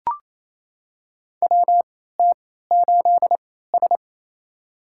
Morse code, heard as a keyed tone, about 28 wpm, spaced as for 16 wpm, 700 Hz. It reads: WT8S
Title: USA Call Signs - 4 Characters - 3x Character Spacing - 28wpm